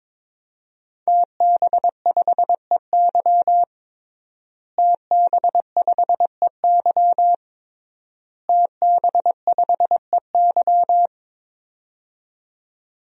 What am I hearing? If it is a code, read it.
Morse code, heard as a keyed tone, about 22 wpm, 700 Hz. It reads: TB5EY TB5EY TB5EY
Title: DXCC-100 Call Signs - 5 Characters - Repeated 3x - 22wpm